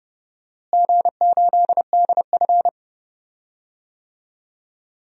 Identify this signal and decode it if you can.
Morse code, heard as a keyed tone, about 30 wpm, 700 Hz. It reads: G8DF
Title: DXCC-100 Call Signs - 4 Characters - Repeated 3x - 30wpm